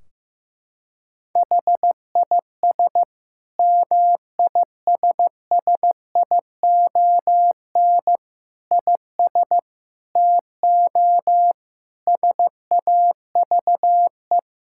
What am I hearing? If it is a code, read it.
Morse code, heard as a keyed tone, about 15 wpm, 700 Hz. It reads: HIS MISSION IS TO SAVE
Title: Sentences from Top 1600 Words 15wpm